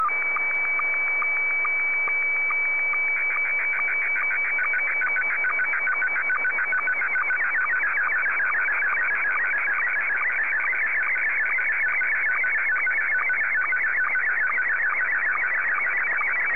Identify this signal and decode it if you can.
SSTV (slow-scan television), Scottie 1.
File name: sstv / S1-VK3